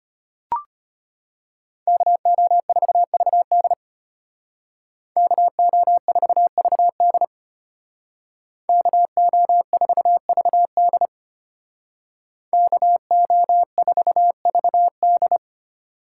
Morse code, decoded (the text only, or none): KO4VD KO4VD KO4VD KO4VD